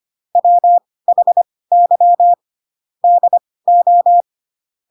Morse code, heard as a keyed tone, about 25 wpm, spaced as for 18 wpm, 700 Hz. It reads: WHY DO
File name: Pinocchio ch3 12 25 ik1hge